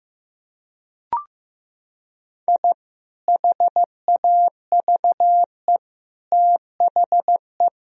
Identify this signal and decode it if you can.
Morse code, heard as a keyed tone, about 15 wpm, 700 Hz. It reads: I HAVE THE
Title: Sentences from Top 1000 Words - Review Part 3 - 15wpm